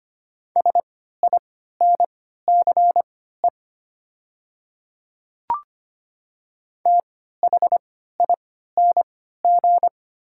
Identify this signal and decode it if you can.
Morse code, heard as a keyed tone, about 25 wpm, spaced as for 14 wpm, 700 Hz. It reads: SINCE THING
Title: 3x Character Spacing - Top 300 Words - 25wpm